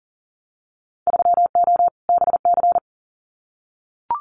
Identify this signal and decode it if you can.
Morse code, heard as a keyed tone, about 42 wpm, 700 Hz. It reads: 3O BC
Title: Winter Field Day Exchanges Encore 40wpm